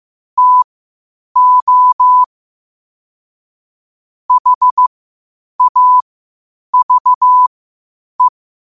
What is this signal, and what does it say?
Morse code, heard as a keyed tone, about 15 wpm, spaced as for 8 wpm, 1000 Hz. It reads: TO HAVE
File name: qotdfiles / 8wpm